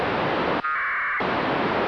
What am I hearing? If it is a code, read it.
Packet radio (AX.25, as in APRS).
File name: satellite / raft / raft psk31 pe1itr 28jan2007 0622UTC